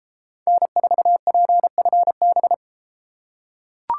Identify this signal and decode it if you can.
Morse code, heard as a keyed tone, about 33 wpm, 700 Hz. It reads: N4PFB